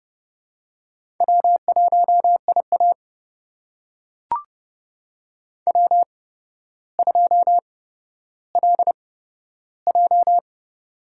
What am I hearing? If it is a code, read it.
Morse code, heard as a keyed tone, about 31 wpm, spaced as for 13 wpm, 700 Hz. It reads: W1IA W2LJ
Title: USA Call Signs - 4 Characters - 8x Character Spacing - 30wpm